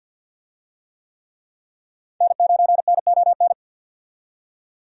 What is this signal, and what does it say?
Morse code, heard as a keyed tone, about 48 wpm, 700 Hz. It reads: N9NON
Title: USA Call Signs - 5 Characters - 50wpm